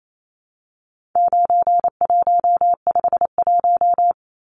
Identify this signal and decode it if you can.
Morse code, heard as a keyed tone, about 28 wpm, 700 Hz. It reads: 9151
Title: Numbers - 4-Digit 28wpm